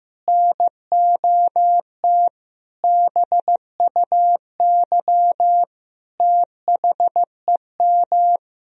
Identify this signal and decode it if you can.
Morse code, heard as a keyed tone, about 15 wpm, 700 Hz. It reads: NOT BUY THEM
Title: Sentences from Top 1000 Words - Review Part 4 - 15wpm